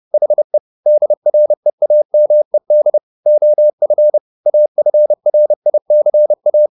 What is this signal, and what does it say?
Morse code, heard as a keyed tone, about 30 wpm, 600 Hz. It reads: HE DREAMED OF AFRICA
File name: the old man and the sea / the old man and the sea 1 30wpm